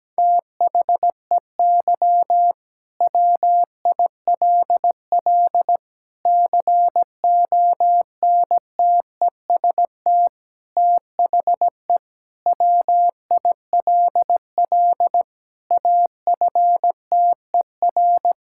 Morse code, heard as a keyed tone, about 17 wpm, 700 Hz. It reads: THEY WILL CONTEST THE WILL AFTER
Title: Sentences from Top 1600 Words 17wpm